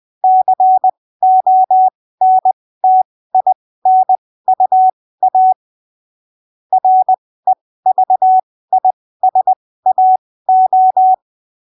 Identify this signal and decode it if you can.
Morse code, heard as a keyed tone, about 20 wpm, spaced as for 15 wpm, 750 Hz. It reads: CONTINUA REVISAO
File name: arquivos / texto / texto 22 20ppm